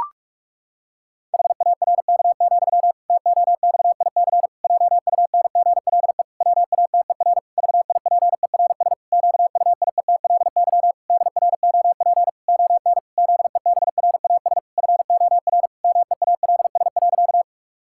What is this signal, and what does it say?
Morse code, heard as a keyed tone, about 46 wpm, 700 Hz. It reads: HARK, TOXIC JUNGLE WATER VIPERS QUIETLY DROP ON ZEBRAS FOR MEALS.
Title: Pangram Challenge v2 - 45wpm